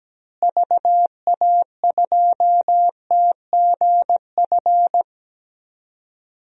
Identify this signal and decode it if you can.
Morse code, heard as a keyed tone, about 17 wpm, 700 Hz. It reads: VA2TGF
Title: DXCC-100 Call Signs - 6 Characters - Repeated 3x - 17wpm